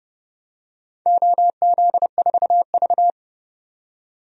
Morse code, heard as a keyed tone, about 31 wpm, 700 Hz. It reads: OZ4V
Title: DXCC-100 Call Signs - 4 Characters - 30wpm